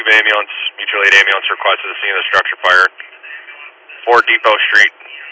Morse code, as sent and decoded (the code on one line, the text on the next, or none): none